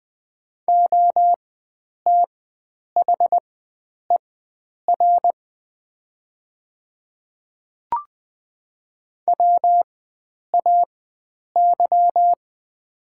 Morse code, heard as a keyed tone, about 20 wpm, spaced as for 9 wpm, 700 Hz. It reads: OTHER WAY
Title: 4x Character Spacing - Top 100 Words - 20wpm